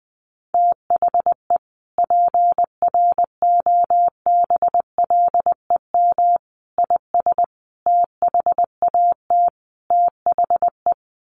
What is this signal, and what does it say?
Morse code, heard as a keyed tone, about 20 wpm, 700 Hz. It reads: THE PROBLEM IS THAT THE
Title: Sets of 5 Words Encore2 20wpm